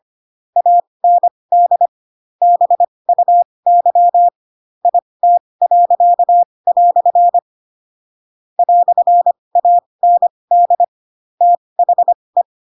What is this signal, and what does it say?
Morse code, heard as a keyed tone, about 25 wpm, spaced as for 20 wpm, 700 Hz. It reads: AND BUY IT." "AND THE
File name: Pinocchio / Pinocchio ch8 15 25 ik1hge